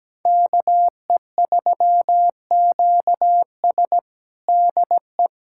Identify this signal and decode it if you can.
Morse code, heard as a keyed tone, about 17 wpm, 700 Hz. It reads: KE3QS DE
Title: USA Callsign DE Callsign - 5 Characters - 17wpm